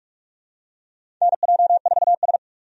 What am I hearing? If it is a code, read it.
Morse code, heard as a keyed tone, about 44 wpm, 700 Hz. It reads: NJ4S
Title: USA Call Signs - 4 Characters - 45wpm